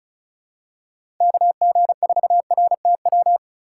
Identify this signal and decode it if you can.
Morse code, heard as a keyed tone, about 36 wpm, 700 Hz. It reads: KG4RTW
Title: USA Call Signs - 6 Characters - 2x Character Spacing - 35wpm